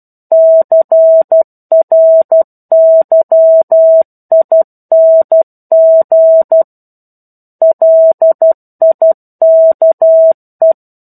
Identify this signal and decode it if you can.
Morse code, heard as a keyed tone, about 12 wpm, 650 Hz. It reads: CRYING LIKE
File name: 01-Chapter-01 12 WPM